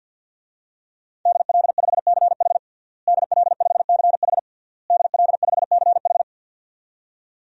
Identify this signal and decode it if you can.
Morse code, heard as a keyed tone, about 52 wpm, 700 Hz. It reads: DL5CH DL5CH DL5CH
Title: DXCC-100 Call Signs - 5 Characters - Repeated 3x - 50wpm